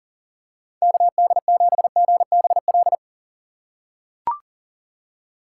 Morse code, heard as a keyed tone, about 40 wpm, 700 Hz. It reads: KD7GBL
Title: USA Call Signs - 6 Characters - 4x Character Spacing - 40wpm